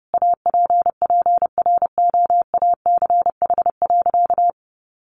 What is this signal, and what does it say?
Morse code, heard as a keyed tone, about 30 wpm, 700 Hz. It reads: APPROACH.
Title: Sentences from Top 1400 Words 30wpm